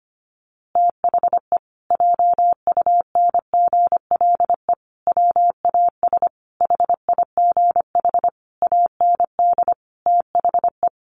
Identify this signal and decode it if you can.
Morse code, heard as a keyed tone, about 25 wpm, 700 Hz. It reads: THE JUNGLE WAS HIGH AND THE
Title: A Sound Of Thunder 25wpm